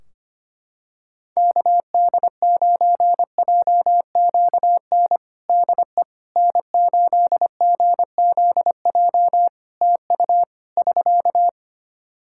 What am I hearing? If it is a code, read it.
Morse code, heard as a keyed tone, about 25 wpm, 700 Hz. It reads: KD9JQN DE N8GZJ TU <SK>
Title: K1USN SST QSO Practice 25wpm